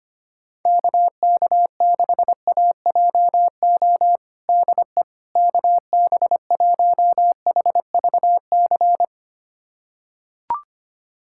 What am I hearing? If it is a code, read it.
Morse code, heard as a keyed tone, about 25 wpm, 700 Hz. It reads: KK6AJO DE KB1HVC
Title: USA Callsign DE Callsign 25wpm